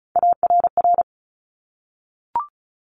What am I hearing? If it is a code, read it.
Morse code, heard as a keyed tone, about 36 wpm, 700 Hz. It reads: ARR
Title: Top Three Letter Combinations in Words 35wpm